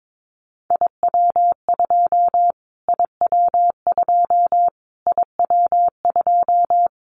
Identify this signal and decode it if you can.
Morse code, heard as a keyed tone, about 22 wpm, 700 Hz. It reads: IW2 IW2 IW2
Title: Callsign Prefixes for Top 25 DXCC - Repeated 3x - 22wpm